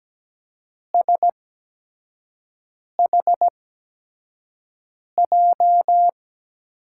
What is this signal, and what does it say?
Morse code, heard as a keyed tone, about 17 wpm, spaced as for 5 wpm, 700 Hz. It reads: SHJ
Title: USA Call Signs - 8x Character Spacing - 17wpm